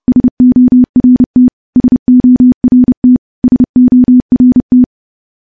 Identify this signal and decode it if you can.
Morse code, heard as a keyed tone, about 30 wpm, 250 Hz. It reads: SORT SORT SORT
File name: morse-training-30WPM